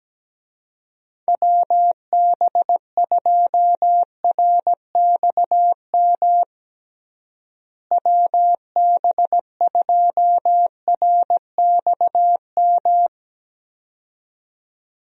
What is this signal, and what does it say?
Morse code, heard as a keyed tone, about 17 wpm, 700 Hz. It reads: WB2RXM WB2RXM
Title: USA Call Signs - 6 Characters - Repeated 2x - 17wpm